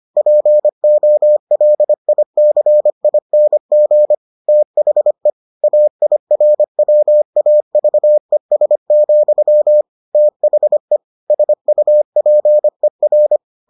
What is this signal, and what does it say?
Morse code, heard as a keyed tone, about 25 wpm, 600 Hz. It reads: POLICING THE AIRWAVES, THE SUPER